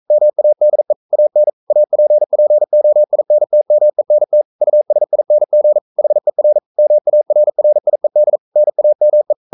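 Morse code, heard as a keyed tone, about 42 wpm, 600 Hz. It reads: MADE AN APPOINTMENT USING HER MARRIED NAME
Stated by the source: Morse code sending of an English novel at 42 wpm.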